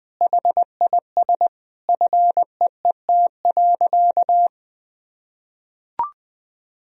Morse code, heard as a keyed tone, about 21 wpm, 700 Hz. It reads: HIS FEET.
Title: Sentences from Top 1000 Words - Review Part 4 - 20wpm